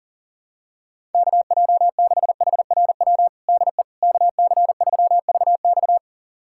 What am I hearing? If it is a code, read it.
Morse code, heard as a keyed tone, about 40 wpm, 700 Hz. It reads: KJ6HRW DE KC3VX
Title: USA Callsign DE Callsign 40wpm